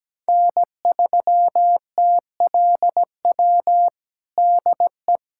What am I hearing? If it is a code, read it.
Morse code, heard as a keyed tone, about 17 wpm, 700 Hz. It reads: N3TLW DE
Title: USA Callsign DE Callsign - 5 Characters - 17wpm